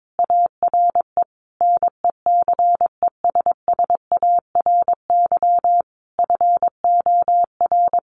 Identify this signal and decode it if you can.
Morse code, heard as a keyed tone, about 22 wpm, 700 Hz. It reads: ARE NECESSARY FOR
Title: Sentences from Top 1300 Words 22wpm